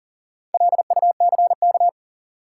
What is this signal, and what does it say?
Morse code, heard as a keyed tone, about 40 wpm, 700 Hz. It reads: LUCK
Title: Sentences from Top 1500 Words - New Word Drill Alt - 40wpm